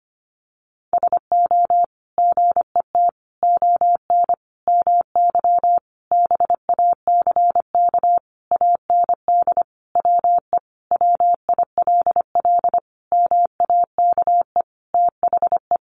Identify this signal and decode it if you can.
Morse code, heard as a keyed tone, about 25 wpm, 700 Hz. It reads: SO GET ON MY BACK AND WE WILL MAKE THE
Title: Sentences from Top 1000 Words 25wpm